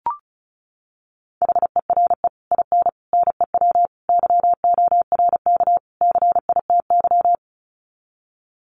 Morse code, heard as a keyed tone, about 34 wpm, 700 Hz. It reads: HERE IN NEW YORK CITY
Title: Sets of 5 Words Encore2 35wpm